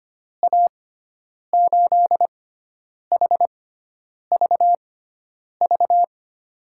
Morse code, heard as a keyed tone, about 25 wpm, spaced as for 9 wpm, 700 Hz. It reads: A8HVV
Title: USA Call Signs - 6 Characters - 6x Character Spacing - 25wpm